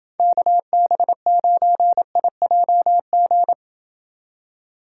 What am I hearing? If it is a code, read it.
Morse code, heard as a keyed tone, about 29 wpm, 700 Hz. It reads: KB9IJG